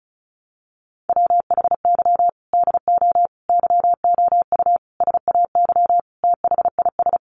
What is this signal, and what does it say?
Morse code, heard as a keyed tone, about 34 wpm, 700 Hz. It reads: WHY DO YOU SAY THIS
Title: Sentences from Top 1100 Words 35wpm